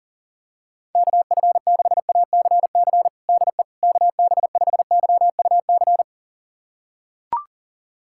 Morse code, heard as a keyed tone, about 40 wpm, 700 Hz. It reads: KF6ACC DE KB5YUC
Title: USA Callsign DE Callsign 40wpm